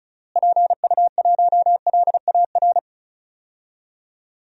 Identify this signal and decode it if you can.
Morse code, heard as a keyed tone, about 36 wpm, 700 Hz. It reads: PU1LAR
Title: DXCC-100 Call Signs - 6 Characters - Repeated 3x - 35wpm